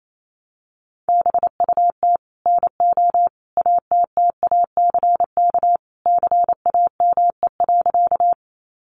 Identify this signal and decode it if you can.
Morse code, heard as a keyed tone, about 28 wpm, 700 Hz. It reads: BUT NO ATTACK CAME.
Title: Sentences from Top 1500 Words 28wpm